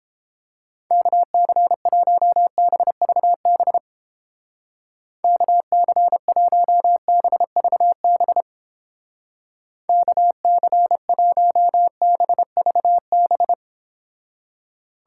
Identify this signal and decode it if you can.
Morse code, heard as a keyed tone, about 29 wpm, 700 Hz. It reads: KC1BVB KC1BVB KC1BVB